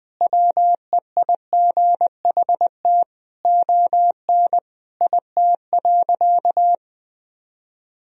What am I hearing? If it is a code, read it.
Morse code, heard as a keyed tone, about 20 wpm, 700 Hz. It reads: WEIGHT ON IT.
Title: Sentences from Top 1000 Words - Review Part 3 - 20wpm